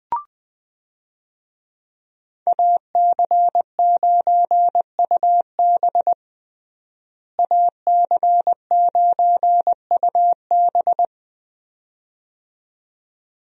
Morse code, heard as a keyed tone, about 20 wpm, 700 Hz. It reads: AC9UB AC9UB
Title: USA Call Signs - 5 Characters - Repeated 2x - 20wpm